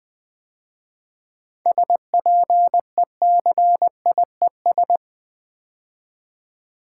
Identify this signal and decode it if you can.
Morse code, heard as a keyed tone, about 20 wpm, 700 Hz. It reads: SPECIES
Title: 7 Letter Words - First 3 Letters Given - 20wpm